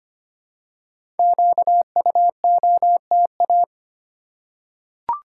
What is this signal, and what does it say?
Morse code, heard as a keyed tone, about 25 wpm, 700 Hz. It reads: QUOTA
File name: 5 Letter Words - First 3 Letters Given - 25wpm